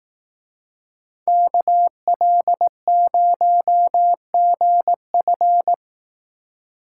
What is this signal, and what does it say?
Morse code, heard as a keyed tone, about 18 wpm, 700 Hz. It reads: KL0GF
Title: Speed Racing - USA 2x2 Call Signs 15wpm